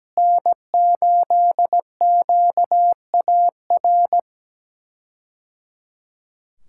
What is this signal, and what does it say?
Morse code, heard as a keyed tone, about 17 wpm, 700 Hz. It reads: N8QAR
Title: K1USN SST QSO Practice - 17wpm